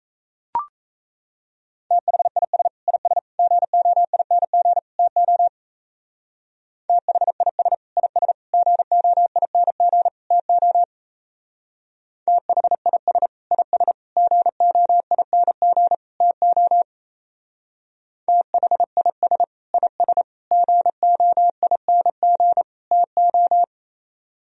Morse code, heard as a keyed tone, about 36 wpm, 700 Hz. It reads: THIS IS GOING TO THIS IS GOING TO THIS IS GOING TO THIS IS GOING TO